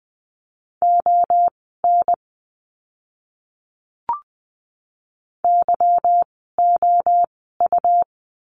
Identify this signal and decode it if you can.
Morse code, heard as a keyed tone, about 20 wpm, spaced as for 14 wpm, 700 Hz. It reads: ON YOU
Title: 192 Lesson 31f - Letter v - Words 20wpm